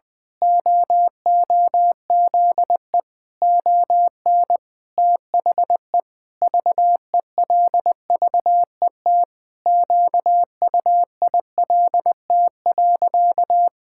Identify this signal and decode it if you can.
Morse code, heard as a keyed tone, about 20 wpm, 700 Hz. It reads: OOZE ON THE VELVET QUILT.
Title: Pangram Challenge v2 - 20wpm